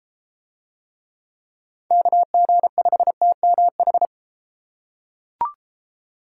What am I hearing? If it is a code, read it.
Morse code, heard as a keyed tone, about 31 wpm, 700 Hz. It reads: KG5TMH